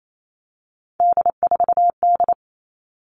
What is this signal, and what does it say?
Morse code, heard as a keyed tone, about 28 wpm, 700 Hz. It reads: D4D
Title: All Cut Numbers - 3-Digit 28wpm